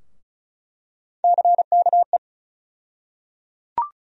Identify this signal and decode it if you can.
Morse code, heard as a keyed tone, about 34 wpm, 700 Hz. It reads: CKE
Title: USA Call Sign Postfixes - 3 Letters 35wpm